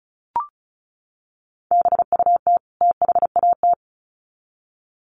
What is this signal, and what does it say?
Morse code, heard as a keyed tone, about 34 wpm, 700 Hz. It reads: BUT THAT
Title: Sets of 2 Words 35wpm